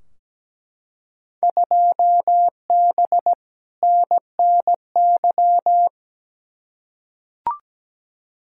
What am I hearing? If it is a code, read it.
Morse code, heard as a keyed tone, about 17 wpm, 700 Hz. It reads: 2B NNY
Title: Field Day Exchanges Encore - 17wpm